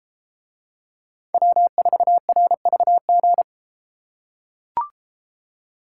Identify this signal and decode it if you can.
Morse code, heard as a keyed tone, about 32 wpm, 700 Hz. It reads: W4RVG